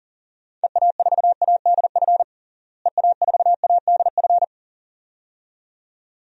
Morse code, heard as a keyed tone, about 40 wpm, 700 Hz. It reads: EA4ADF EA4ADF
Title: DXCC-100 Call Signs - 6 Characters - Repeated 2x - 40wpm